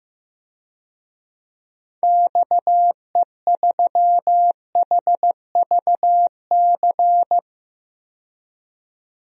Morse code, heard as a keyed tone, about 15 wpm, 700 Hz. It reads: XE3HVC
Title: DXCC-100 Call Signs - 6 Characters - 15wpm